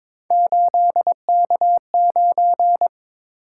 Morse code, heard as a keyed tone, about 27 wpm, 700 Hz. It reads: 8K9
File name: Speed-Racing - Three Letter-Number Combinations - 22wpm